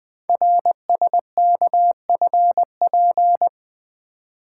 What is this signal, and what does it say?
Morse code, heard as a keyed tone, about 20 wpm, 700 Hz. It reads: RSKFP
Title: Code Groups - 5 Letters - 20wpm